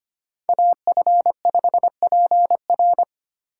Morse code, heard as a keyed tone, about 25 wpm, 700 Hz. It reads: AF5PR
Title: USA Call Signs - 3x Character Spacing - 25wpm